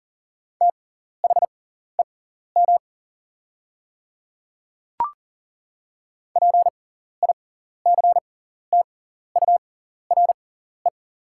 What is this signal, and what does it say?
Morse code, heard as a keyed tone, about 40 wpm, spaced as for 14 wpm, 700 Hz. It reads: THEM PICTURE
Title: 6x Character Spacing - Top 200 Words - 40wpm